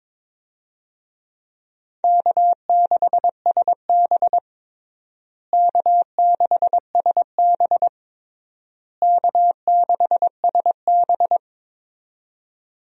Morse code, heard as a keyed tone, about 22 wpm, 700 Hz. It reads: K6SB K6SB K6SB
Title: USA Call Signs - 4 Characters - Repeated 3x - 22wpm